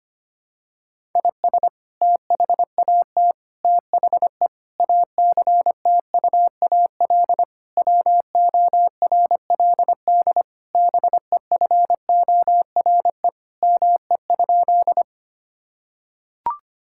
Morse code, heard as a keyed tone, about 25 wpm, 700 Hz. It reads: IS THAT THE ACTUAL WORLD BEFORE ME?
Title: Sentences from Top 1000 Words - Review Part 5 - 25wpm